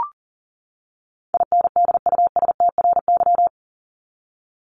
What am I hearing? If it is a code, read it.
Morse code, heard as a keyed tone, about 40 wpm, 700 Hz. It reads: INDUSTRY
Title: Top 501-750 Words 40wpm